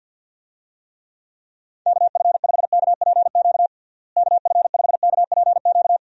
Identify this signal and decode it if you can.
Morse code, heard as a keyed tone, about 52 wpm, 700 Hz. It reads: KF5KPQ KF5KPQ
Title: USA Call Signs - 6 Characters - Repeated 2x - 50wpm